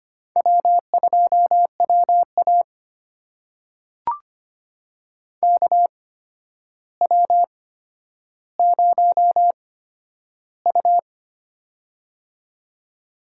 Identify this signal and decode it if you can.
Morse code, heard as a keyed tone, about 25 wpm, spaced as for 11 wpm, 700 Hz. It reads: W2WA KW0U
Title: USA Call Signs - 4 Characters - 8x Character Spacing - 25wpm